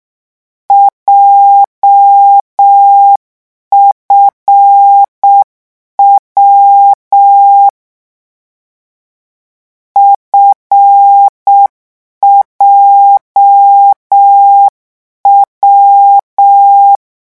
Morse code, heard as a keyed tone, about 6 wpm, 800 Hz. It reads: JFW FJW